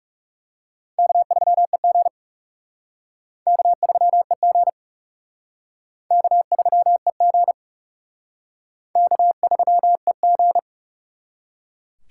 Morse code, heard as a keyed tone, about 40 wpm, 700 Hz. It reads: K3EG K3EG K3EG K3EG